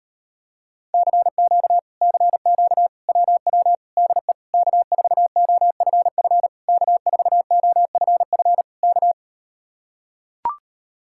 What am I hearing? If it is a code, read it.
Morse code, heard as a keyed tone, about 38 wpm, 700 Hz. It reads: CQ CQ WW DE K4OFF K4OFF K